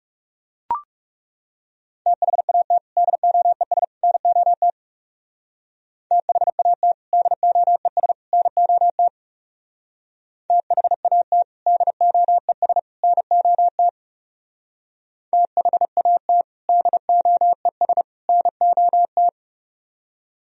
Morse code, heard as a keyed tone, about 38 wpm, 700 Hz. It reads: THAT DOES NOT THAT DOES NOT THAT DOES NOT THAT DOES NOT